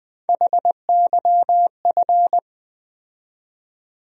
Morse code, heard as a keyed tone, about 20 wpm, 700 Hz. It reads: HYF
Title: USA Call Sign Postfixes - 3 Letters 20wpm